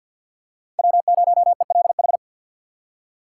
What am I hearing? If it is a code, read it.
Morse code, heard as a keyed tone, about 53 wpm, 690 Hz. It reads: W0ELH
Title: USA Call Signs - 6x Character Spacing - 50wpm